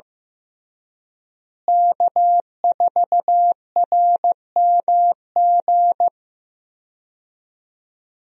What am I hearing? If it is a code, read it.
Morse code, heard as a keyed tone, about 15 wpm, 700 Hz. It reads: K4RMG
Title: USA Call Signs - 5 Characters - Repeated 3x - 15wpm